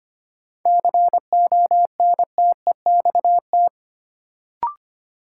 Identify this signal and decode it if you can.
Morse code, heard as a keyed tone, about 25 wpm, 700 Hz. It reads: CONTEXT
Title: Sets of 4 Words - New Word Drill - 25wpm